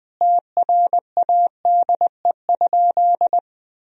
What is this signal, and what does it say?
Morse code, heard as a keyed tone, about 20 wpm, 700 Hz. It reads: TRADE?
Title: Sentences from Top 1000 Words - Review Part 4 - 20wpm